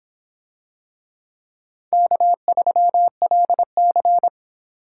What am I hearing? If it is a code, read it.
Morse code, heard as a keyed tone, about 26 wpm, 700 Hz. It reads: K3LC